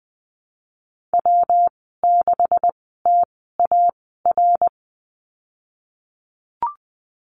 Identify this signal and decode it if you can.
Morse code, heard as a keyed tone, about 20 wpm, spaced as for 14 wpm, 700 Hz. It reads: W6TAR
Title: 154 Lesson 25g - Letter p - Call Signs 20wpm